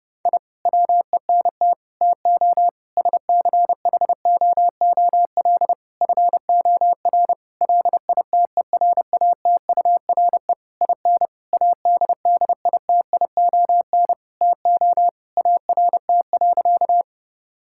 Morse code, heard as a keyed tone, about 32 wpm, 700 Hz. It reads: I WENT TO SCHOOL FOR LITERATURE IN ADDITION TO ART.
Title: Sentences from Top 1000 Words - Review Part 6 - 30wpm